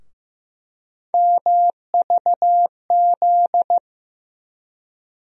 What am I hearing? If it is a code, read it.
Morse code, heard as a keyed tone, about 15 wpm, 700 Hz. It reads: MVZ
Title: USA Call Sign Postfixes - 3 Letters 15wpm